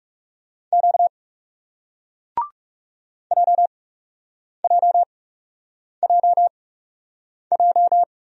Morse code, heard as a keyed tone, about 37 wpm, spaced as for 8 wpm, 700 Hz. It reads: Q JJJJ
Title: Speed-Racing Single Letter-Number 30wpm